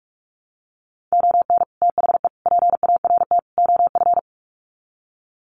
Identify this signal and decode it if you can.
Morse code, heard as a keyed tone, about 44 wpm, 700 Hz. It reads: ON THE PART OF
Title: Sets of 4 Words 45wpm